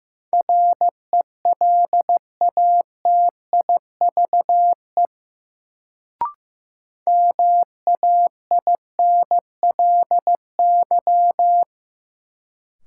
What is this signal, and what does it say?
Morse code, heard as a keyed tone, about 16 wpm, 700 Hz. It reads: RELATIVE MAINLY
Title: Sentences from Top 1400 Words - New Word Drill - 15wpm